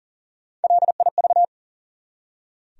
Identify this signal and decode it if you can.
Morse code, heard as a keyed tone, about 40 wpm, 700 Hz. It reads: LIV
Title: USA Common Names - Max 3 Letters - 40wpm